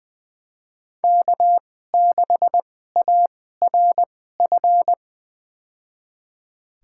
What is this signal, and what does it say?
Morse code, heard as a keyed tone, about 20 wpm, spaced as for 14 wpm, 700 Hz. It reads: K6ARF
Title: USA Call Signs - 5 Characters - 2x Character Spacing - 20wpm